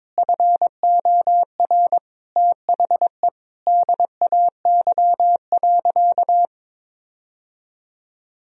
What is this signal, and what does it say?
Morse code, heard as a keyed tone, about 22 wpm, 700 Hz. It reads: FOR THE DAY.
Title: Sentences from Top 1000 Words - Review Part 4 - 22wpm